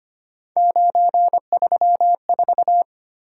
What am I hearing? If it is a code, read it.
Morse code, heard as a keyed tone, about 25 wpm, 700 Hz. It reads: 934
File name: Three Letter-Number Combinations - 25wpm